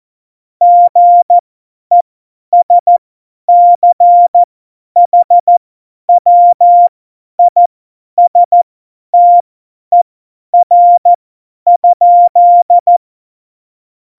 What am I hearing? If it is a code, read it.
Morse code, heard as a keyed tone, about 14 wpm, spaced as for 10 wpm, 700 Hz. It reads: GESCHWISTER?
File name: Morsekurs / Witze / Witz 03. November 2020, 50 F70